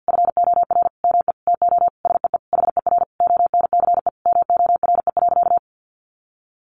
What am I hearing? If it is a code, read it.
Morse code, heard as a keyed tone, about 50 wpm, 700 Hz. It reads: FOR ME TO SEE HER ONCE MORE.
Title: Sentences from Top 500 Words 50wpm